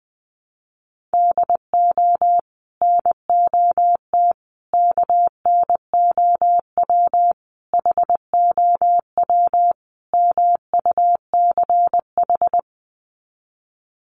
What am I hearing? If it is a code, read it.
Morse code, heard as a keyed tone, about 20 wpm, 700 Hz. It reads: DO NOT KNOW HOW MUCH
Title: Sets of 5 Words Encore 20wpm